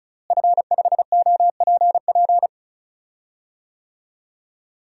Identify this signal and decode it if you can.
Morse code, heard as a keyed tone, about 36 wpm, 700 Hz. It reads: F5OPP
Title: DXCC-100 Call Signs - 5 Characters - Repeated 2x - 35wpm